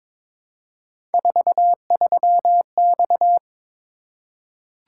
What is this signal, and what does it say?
Morse code, heard as a keyed tone, about 22 wpm, 700 Hz. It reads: 43X
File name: practice / 22wpm / Three Letter-Number Combinations - 22wpm